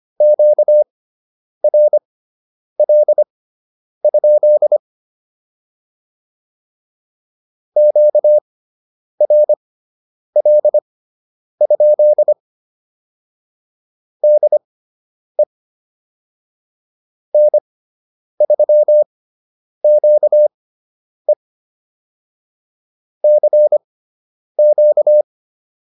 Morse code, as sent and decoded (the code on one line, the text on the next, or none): --.- .-. .-.. ..--.. / --.- .-. .-.. ..--.. / -.. / -. ...-- --.- . / -.-. --.-
QRL? QRL? DE N3QE CQ